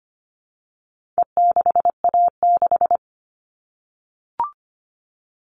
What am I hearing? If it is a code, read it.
Morse code, heard as a keyed tone, about 25 wpm, 700 Hz. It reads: E6A6
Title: All Cut Numbers - 4-Digit 25wpm